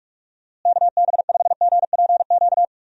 Morse code, heard as a keyed tone, about 47 wpm, 700 Hz. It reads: KB5GPQ
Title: USA Call Signs - 6 Characters - Repeated 3x - 45wpm